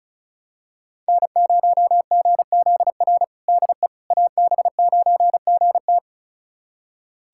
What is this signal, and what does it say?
Morse code, heard as a keyed tone, about 36 wpm, 700 Hz. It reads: N0GZR DE AB9GT
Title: USA Callsign DE Callsign - 5 Characters - 35wpm